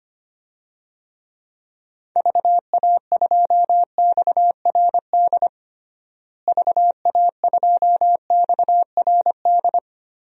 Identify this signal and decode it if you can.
Morse code, heard as a keyed tone, about 25 wpm, 700 Hz. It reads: VA2XRD VA2XRD
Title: DXCC-100 Call Signs - 6 Characters - Repeated 3x - 25wpm